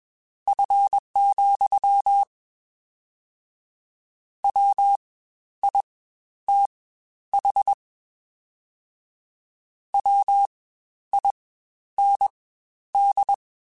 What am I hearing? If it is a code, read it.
Morse code, heard as a keyed tone, about 21 wpm, spaced as for 11 wpm, 800 Hz. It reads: F, WITH WIND
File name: Spaced Characters / Spaced Characters-mp3s / Weather5 1 sp @20